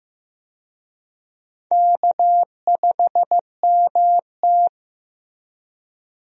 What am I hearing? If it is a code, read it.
Morse code, heard as a keyed tone, about 15 wpm, 700 Hz. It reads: K5MT
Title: USA Call Signs - 4 Characters - Repeated 2x - 15wpm